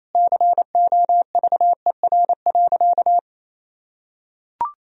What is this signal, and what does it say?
Morse code, heard as a keyed tone, about 27 wpm, 700 Hz. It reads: COVER.
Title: Sentences from Top 1000 Words - Review Part 1 - 28wpm